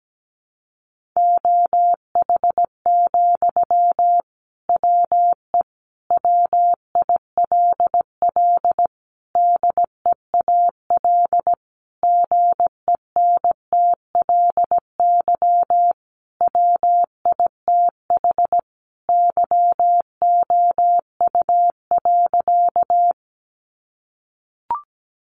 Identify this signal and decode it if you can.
Morse code, heard as a keyed tone, about 17 wpm, 700 Hz. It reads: OH, WE WILL DEAL GENTLY WITH YOU.
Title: Sentences from Top 1500 Words 17wpm